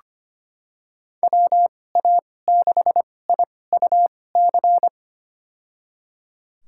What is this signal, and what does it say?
Morse code, heard as a keyed tone, about 25 wpm, spaced as for 18 wpm, 700 Hz. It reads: WA6IUC
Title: USA Call Signs - 2x Character Spacing - 25wpm